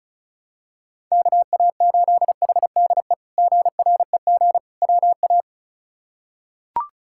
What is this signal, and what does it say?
Morse code, heard as a keyed tone, about 34 wpm, 700 Hz. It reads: KA8HDE GREG WA
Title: K1USN SST QSO Practice - 35wpm